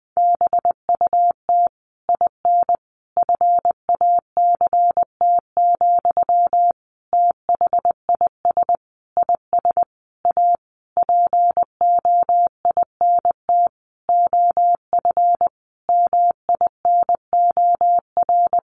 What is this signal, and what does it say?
Morse code, heard as a keyed tone, about 20 wpm, 700 Hz. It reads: BUT IN FACT, THIS IS A POINT OF MINOR
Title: Sentences from Top 1500 Words 20wpm